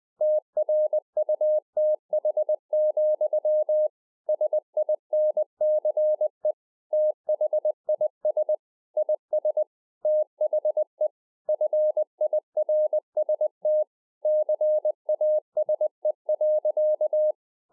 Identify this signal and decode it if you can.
Morse code, heard as a keyed tone, about 20 wpm, 610 Hz. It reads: TRUTH, SINCE THIS IS THE FIRST CASE.
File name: 20WPM